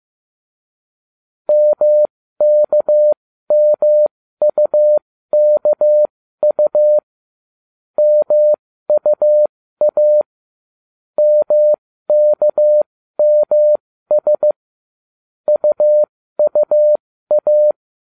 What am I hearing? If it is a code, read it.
Morse code, heard as a keyed tone, about 15 wpm, spaced as for 13 wpm, 600 Hz. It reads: MKMUKU MUA MKMS UUA